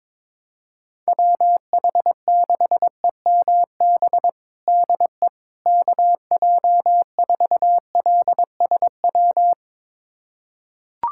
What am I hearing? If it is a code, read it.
Morse code, heard as a keyed tone, about 22 wpm, 700 Hz. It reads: WH6EMB DE KJ4LSW
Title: USA Callsign DE Callsign 22wpm